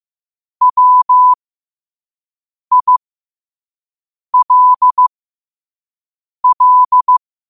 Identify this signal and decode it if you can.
Morse code, heard as a keyed tone, about 15 wpm, spaced as for 5 wpm, 1000 Hz. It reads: WILL
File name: qotdfiles / 5wpm